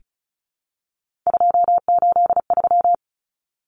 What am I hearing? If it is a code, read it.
Morse code, heard as a keyed tone, about 34 wpm, 700 Hz. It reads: 283
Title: Numbers - 3-Digit 35wpm